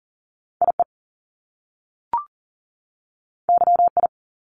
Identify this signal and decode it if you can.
Morse code, heard as a keyed tone, about 41 wpm, 700 Hz. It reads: IE YI
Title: Top Two Letter Combinations in Words 40wpm